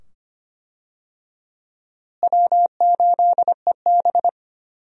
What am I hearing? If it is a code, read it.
Morse code, heard as a keyed tone, about 25 wpm, 700 Hz. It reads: W8EB
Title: USA Call Signs - 4 Characters - Repeated 2x - 25wpm